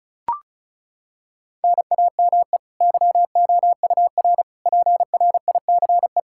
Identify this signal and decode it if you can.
Morse code, heard as a keyed tone, about 36 wpm, 700 Hz. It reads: NAME YOUR PRICE
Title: Sentences from Top 1000 Words - Review Part 5 - 35wpm